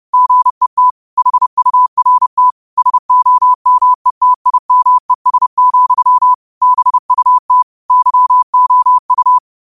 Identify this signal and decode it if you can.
Morse code, heard as a keyed tone, about 30 wpm, 1000 Hz. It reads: GET HURT SOMETIMES, BUT YOU
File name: qotdfiles / 30wpm